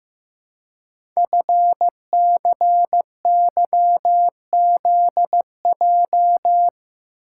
Morse code, heard as a keyed tone, about 15 wpm, 700 Hz. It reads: FCYZJ
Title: Code Groups - 5 Letters - 15wpm